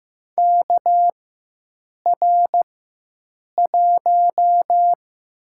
Morse code, heard as a keyed tone, about 15 wpm, 700 Hz. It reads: K R 1
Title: USA Call Signs - 4 Characters - 4x Character Spacing - 15wpm